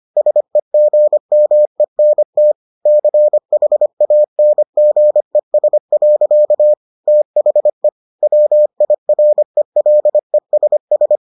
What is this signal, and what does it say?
Morse code, heard as a keyed tone, about 25 wpm, 600 Hz. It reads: SEGMENT CHANGES. THE WIRELESS